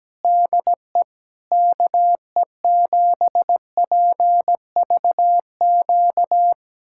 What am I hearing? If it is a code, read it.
Morse code, heard as a keyed tone, about 17 wpm, 700 Hz. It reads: DE KE7PVQ
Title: USA Callsign DE Callsign 17wpm